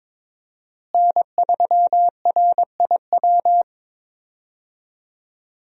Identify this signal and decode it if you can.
Morse code, heard as a keyed tone, about 22 wpm, 700 Hz. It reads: N3RIW
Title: USA Call Signs - 5 Characters - Repeated 3x - 22wpm